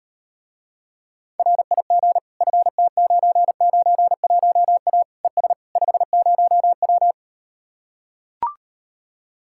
Morse code, heard as a keyed tone, about 39 wpm, 700 Hz. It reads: RIG FT991A ES 50W